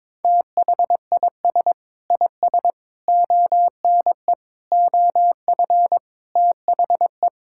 Morse code, heard as a keyed tone, about 22 wpm, 700 Hz. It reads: THIS IS ONE OF THE
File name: practice / 22wpm / Sets of 7 Words - 22wpm